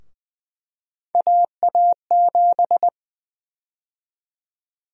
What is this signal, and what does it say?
Morse code, heard as a keyed tone, about 20 wpm, 700 Hz. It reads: AA7
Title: USA Call Sign Prefixes - 20wpm